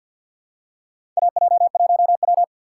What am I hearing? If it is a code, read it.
Morse code, heard as a keyed tone, about 53 wpm, 700 Hz. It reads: AJ1W
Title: USA Call Signs - 4 Characters - 6x Character Spacing - 50wpm